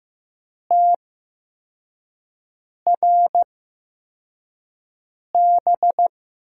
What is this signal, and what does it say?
Morse code, heard as a keyed tone, about 15 wpm, spaced as for 4 wpm, 700 Hz. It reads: TRB
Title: USA Call Signs - 6 Characters - 8x Character Spacing - 15wpm